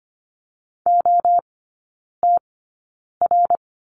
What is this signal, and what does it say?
Morse code, heard as a keyed tone, about 25 wpm, spaced as for 9 wpm, 700 Hz. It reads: OTR
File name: course / 25:10wpm / 093 Lesson 16e - Letter c - Three Characters 25wpm